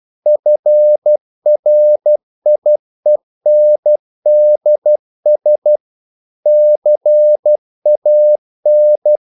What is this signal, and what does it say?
Morse code, heard as a keyed tone, about 12 wpm, 600 Hz. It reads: FRIENDS CAN
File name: CW / Practice / If12